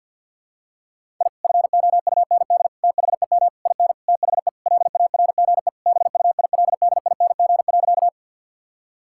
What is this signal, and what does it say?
Morse code, heard as a keyed tone, about 50 wpm, 700 Hz. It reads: I FOUND THEM IN THE LARGE BUILDING.
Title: Sentences from Top 1000 Words - Review Part 5 - 50wpm